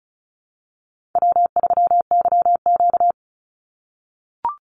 Morse code, heard as a keyed tone, about 36 wpm, 700 Hz. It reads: W3YQ
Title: Call Signs - 35wpm